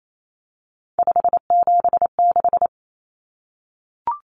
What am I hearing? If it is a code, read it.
Morse code, heard as a keyed tone, about 28 wpm, 700 Hz. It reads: 576
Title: Common Cut Numbers - 3-Digit 28wpm